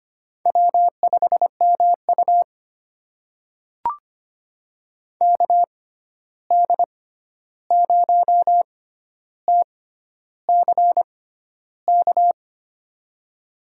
Morse code, heard as a keyed tone, about 25 wpm, spaced as for 11 wpm, 700 Hz. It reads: W5MU KD0TCK
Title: USA Call Signs - 6x Character Spacing - 25wpm